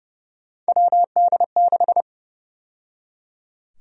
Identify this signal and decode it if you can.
Morse code, heard as a keyed tone, about 30 wpm, 700 Hz. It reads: WD6
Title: USA Call Sign Prefixes - 30wpm